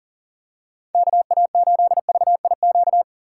Morse code, heard as a keyed tone, about 40 wpm, 700 Hz. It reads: KA8VIQ
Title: USA Call Signs - 6 Characters - 3x Character Spacing - 40wpm